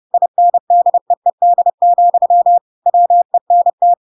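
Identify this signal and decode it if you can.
Morse code, heard as a keyed tone, about 30 wpm, 700 Hz. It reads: INDEED, WENT
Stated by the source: Chapter 2 of The Adventures of Pinocchio read as a book in Morse code (CW) at 30 wpm, by ik1hge.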